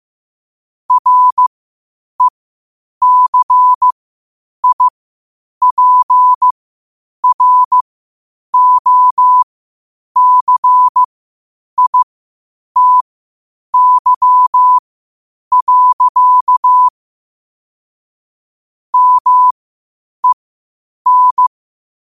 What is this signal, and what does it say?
Morse code, heard as a keyed tone, about 15 wpm, spaced as for 8 wpm, 1000 Hz. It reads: RECIPROCITY. MEN
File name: qotdfiles / 8wpm